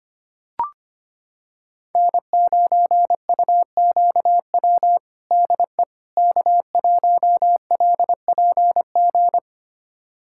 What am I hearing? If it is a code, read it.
Morse code, heard as a keyed tone, about 25 wpm, 700 Hz. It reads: N9UQW DE K1LPG
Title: USA Callsign DE Callsign - 5 Characters - 25wpm